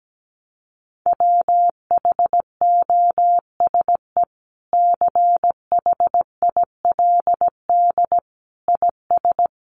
Morse code, heard as a keyed tone, about 17 wpm, 700 Hz. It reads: WHOSE CHILD IS
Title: Sentences from Top 1200 Words 17wpm